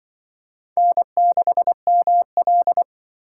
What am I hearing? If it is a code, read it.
Morse code, heard as a keyed tone, about 24 wpm, 700 Hz. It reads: N6ML
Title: Speed Racing - USA 1x2 Call Signs 20wpm